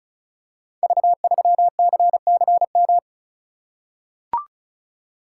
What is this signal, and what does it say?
Morse code, heard as a keyed tone, about 36 wpm, 690 Hz. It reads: V3CCM
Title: Code Groups - 5 Letter:Numbers - 35wpm